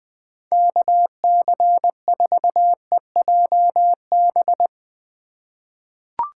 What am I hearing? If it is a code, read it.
Morse code, heard as a keyed tone, about 20 wpm, 700 Hz. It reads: KC4EJB
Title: USA Call Signs - 6 Characters - 8x Character Spacing - 20wpm